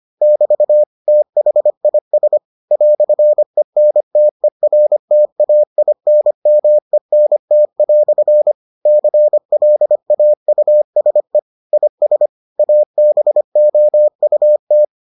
Morse code, heard as a keyed tone, about 25 wpm, 600 Hz. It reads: <BT> THIS "ENTERTAINMENT" CLAUSE IS ABOUT